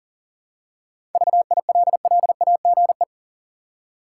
Morse code, heard as a keyed tone, about 40 wpm, 700 Hz. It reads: VILLAGE
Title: Seven Letter Words - 40wpm